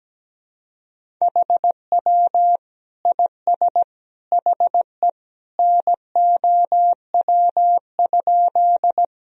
Morse code, heard as a keyed tone, about 17 wpm, 700 Hz. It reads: HW IS HE NOW?
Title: Sentences from Top 100 Words and Abbreviations 17wpm